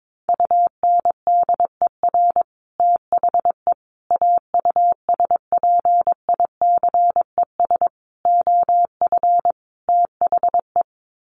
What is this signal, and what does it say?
Morse code, heard as a keyed tone, about 22 wpm, 700 Hz. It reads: UNDER THE AUSPICES OF THE
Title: Sets of 5 Words Encore2 22wpm